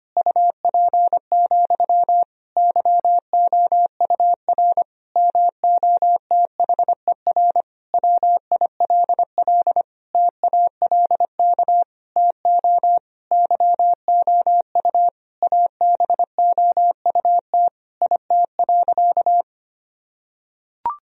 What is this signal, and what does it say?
Morse code, heard as a keyed tone, about 25 wpm, 700 Hz. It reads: UP, YOUR MOTHER WILL TALK TO YOU ABOUT IT.
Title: Sentences from Top 1000 Words - Review Part 1 - 25wpm